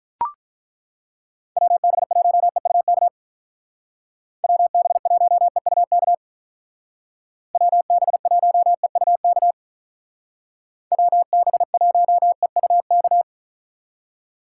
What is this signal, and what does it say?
Morse code, heard as a keyed tone, about 42 wpm, 700 Hz. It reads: WB1EUK WB1EUK WB1EUK WB1EUK
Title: Speed Racing - USA 2x3 Call Signs 35wpm